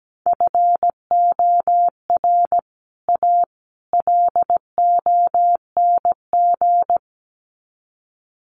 Morse code, heard as a keyed tone, about 17 wpm, 700 Hz. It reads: FOR A LONG
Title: Sets of 3 Words 17wpm